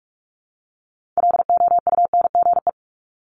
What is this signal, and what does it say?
Morse code, heard as a keyed tone, about 46 wpm, 700 Hz. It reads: LOUNGE
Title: Multi-letter Phonemes Repeated 3x 45wpm